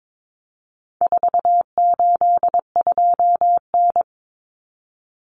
Morse code, heard as a keyed tone, about 22 wpm, 700 Hz. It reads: 482N
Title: Common Cut Numbers - 4-Digit 22wpm